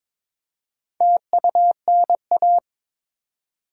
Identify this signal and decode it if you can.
Morse code, heard as a keyed tone, about 22 wpm, 700 Hz. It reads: TUNA
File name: Difficult Words with Hints - 22wpm